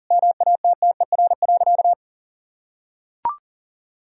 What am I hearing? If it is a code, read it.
Morse code, heard as a keyed tone, about 37 wpm, 700 Hz. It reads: MATTER.
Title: Sentences from Top 1000 Words - Review Part 1 - 40wpm